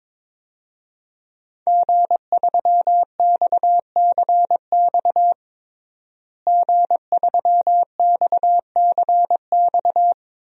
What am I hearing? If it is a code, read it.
Morse code, heard as a keyed tone, about 22 wpm, 700 Hz. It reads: G3XCX G3XCX
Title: DXCC-100 Call Signs - 5 Characters - Repeated 3x - 22wpm